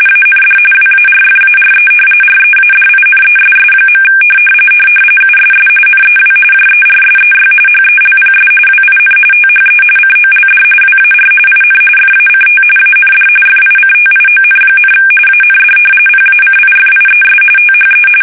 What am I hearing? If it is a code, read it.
RTTY, about 45.45 baud. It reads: AND WARFARE READINESS. USS NEW JERSEY HELPS WIN THE COLD WAR IN FACT AND FICTION, AND PACIFIES THE
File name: fsk850 / battleshipnj